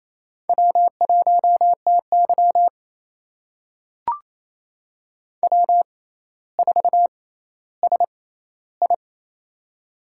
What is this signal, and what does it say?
Morse code, heard as a keyed tone, about 28 wpm, spaced as for 14 wpm, 700 Hz. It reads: W1TY W4SI
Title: USA Call Signs - 4 Characters - 6x Character Spacing - 28wpm